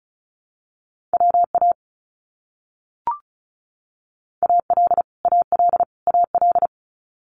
Morse code, heard as a keyed tone, about 34 wpm, 700 Hz. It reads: WA AL AL AL
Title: US States Repeated 3x 35wpm